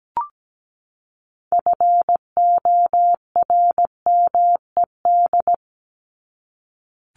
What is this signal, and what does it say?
Morse code, heard as a keyed tone, about 17 wpm, 700 Hz. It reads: FORMED
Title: ED Suffix - Top 100 Words with Base Words 17wpm